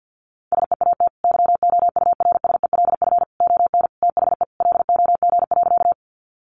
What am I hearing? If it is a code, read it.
Morse code, heard as a keyed tone, about 50 wpm, 700 Hz. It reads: SEAT YOURSELF ON THE LOG.
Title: Sentences from Top 1500 Words 50wpm